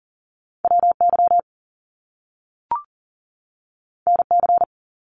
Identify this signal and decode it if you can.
Morse code, heard as a keyed tone, about 39 wpm, 700 Hz. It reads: WY NC
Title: US States 40wpm